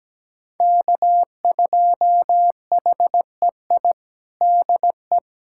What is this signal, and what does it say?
Morse code, heard as a keyed tone, about 17 wpm, 700 Hz. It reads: K2HEI DE
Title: USA Callsign DE Callsign - 5 Characters - 17wpm